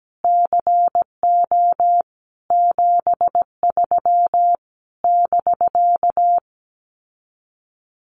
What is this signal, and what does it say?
Morse code, heard as a keyed tone, about 17 wpm, 700 Hz. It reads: CO 73 <BK>
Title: Common QSO Phrases Encore 17wpm